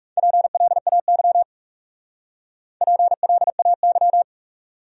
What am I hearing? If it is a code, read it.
Morse code, heard as a keyed tone, about 36 wpm, 700 Hz. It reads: PLAY PLAY
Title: Speed-Racing Top 200 Words 30wpm